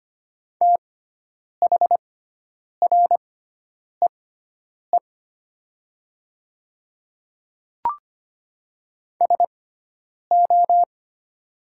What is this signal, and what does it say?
Morse code, heard as a keyed tone, about 25 wpm, spaced as for 9 wpm, 700 Hz. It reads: THREE SO
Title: 6x Character Spacing - Top 200 Words - 25wpm